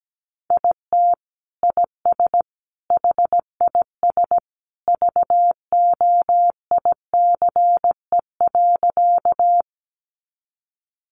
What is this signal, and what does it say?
Morse code, heard as a keyed tone, about 17 wpm, 700 Hz. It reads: IT IS HIS VOICE.
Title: Sentences from Top 500 Words 17wpm